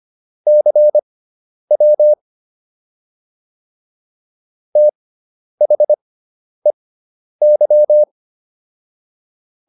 Morse code, heard as a keyed tone, about 25 wpm, spaced as for 10 wpm, 600 Hz. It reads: CW THEY
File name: SS213 10